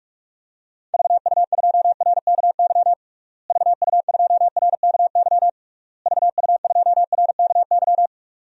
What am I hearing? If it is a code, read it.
Morse code, heard as a keyed tone, about 43 wpm, 700 Hz. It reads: VU2RKY VU2RKY VU2RKY
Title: DXCC-100 Call Signs - 6 Characters - Repeated 3x - 45wpm